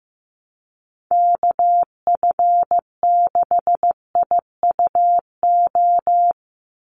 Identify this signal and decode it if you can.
Morse code, heard as a keyed tone, about 15 wpm, 700 Hz. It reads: KF6IUO
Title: USA Call Signs 15wpm